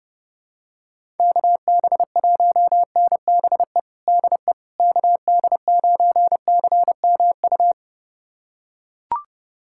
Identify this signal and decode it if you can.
Morse code, heard as a keyed tone, about 30 wpm, 700 Hz. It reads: KB1NBE DE KD9CMU
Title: USA Callsign DE Callsign - 6 Characters - 30wpm